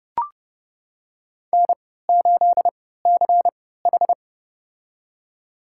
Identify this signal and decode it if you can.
Morse code, heard as a keyed tone, about 29 wpm, spaced as for 17 wpm, 700 Hz. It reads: N8CH
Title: USA Call Signs - 3x Character Spacing - 30wpm